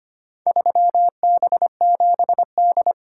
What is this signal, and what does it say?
Morse code, heard as a keyed tone, about 25 wpm, 700 Hz. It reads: 3B7D
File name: practice / 25wpm / Four Letter-Number Combinations - 25wpm